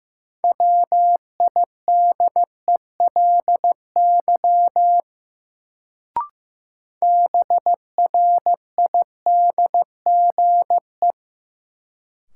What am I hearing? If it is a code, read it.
Morse code, heard as a keyed tone, about 15 wpm, 700 Hz. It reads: WIDELY BRIDGE
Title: Sentences from Top 1200 Words - New Word Drill Alt - 15wpm